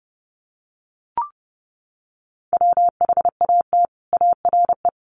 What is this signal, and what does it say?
Morse code, heard as a keyed tone, about 30 wpm, 700 Hz. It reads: WHAT ARE
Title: Sets of 4 Words 30wpm